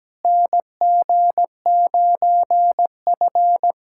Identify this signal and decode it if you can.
Morse code, heard as a keyed tone, about 17 wpm, 700 Hz. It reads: NG9F
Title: USA Callsign DE Callsign - 4 Characters - 17wpm